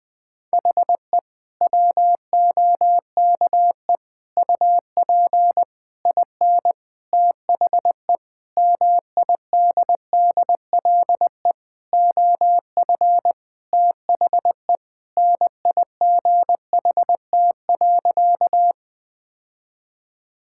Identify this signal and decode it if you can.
Morse code, heard as a keyed tone, about 21 wpm, 700 Hz. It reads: HE WOKE UP IN THE MIDDLE OF THE NIGHT.
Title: Sentences with 6-Word Phrases - 20wpm